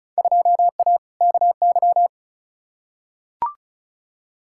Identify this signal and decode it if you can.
Morse code, heard as a keyed tone, about 34 wpm, 700 Hz. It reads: 2A KY
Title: Field Day Exchanges - 35wpm